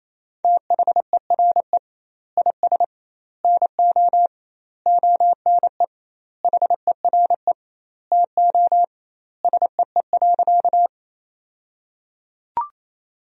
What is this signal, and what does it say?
Morse code, heard as a keyed tone, about 28 wpm, 700 Hz. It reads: THERE IS NO ONE HERE TO SEE.
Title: 2x Word-Spacing - Sentences from Top 200 Words - 28wpm